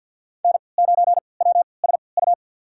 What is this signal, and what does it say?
Morse code, heard as a keyed tone, about 53 wpm, spaced as for 29 wpm, 700 Hz. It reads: N9WSU
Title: USA Call Signs - 5 Characters - 3x Character Spacing - 50wpm